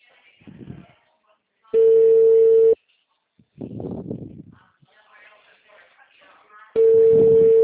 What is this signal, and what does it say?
Ringback tone (China, 450 Hz), with speech.